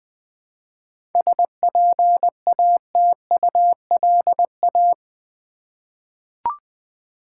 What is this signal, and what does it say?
Morse code, heard as a keyed tone, about 20 wpm, 700 Hz. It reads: SPATULA
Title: Easy Words with Hints - 20wpm